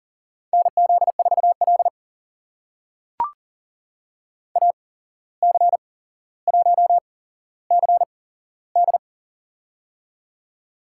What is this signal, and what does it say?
Morse code, heard as a keyed tone, about 44 wpm, spaced as for 16 wpm, 700 Hz. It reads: NZ4L AC1CD
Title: USA Call Signs - 8x Character Spacing - 40wpm